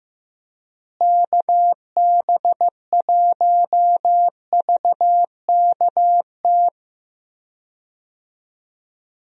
Morse code, heard as a keyed tone, about 15 wpm, 700 Hz. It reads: KB1VKT
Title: USA Call Signs - 6 Characters - Repeated 3x - 15wpm